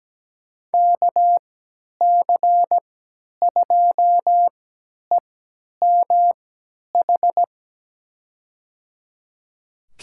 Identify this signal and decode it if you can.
Morse code, heard as a keyed tone, about 17 wpm, spaced as for 10 wpm, 700 Hz. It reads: KC2EMH
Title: USA Call Signs - 6 Characters - 3x Character Spacing - 17wpm